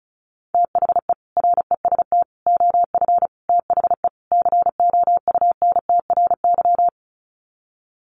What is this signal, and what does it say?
Morse code, heard as a keyed tone, about 34 wpm, 700 Hz. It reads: THE REST OF THE COUNTRY
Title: Sets of 5 Words 35wpm